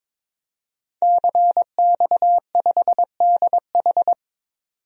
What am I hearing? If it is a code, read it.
Morse code, heard as a keyed tone, about 22 wpm, 700 Hz. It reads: CX5DH
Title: DXCC-100 Call Signs - 5 Characters - 22wpm